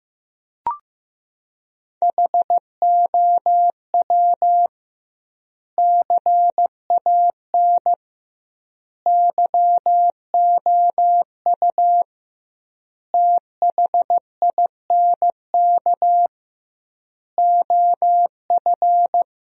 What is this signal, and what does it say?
Morse code, heard as a keyed tone, about 15 wpm, 700 Hz. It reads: HOW CAN YOU THINK OF
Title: 2x Word-Spacing - Sentences from Top 200 Words - 15wpm